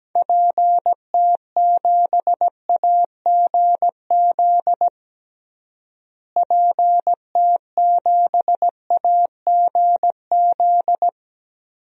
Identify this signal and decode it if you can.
Morse code, heard as a keyed tone, about 17 wpm, 700 Hz. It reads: PT7AGZ PT7AGZ
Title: DXCC-100 Call Signs - 6 Characters - Repeated 3x - 17wpm